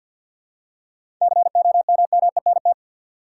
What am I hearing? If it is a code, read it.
Morse code, heard as a keyed tone, about 50 wpm, 700 Hz. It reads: COMMENT
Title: Seven Letter Words - 50wpm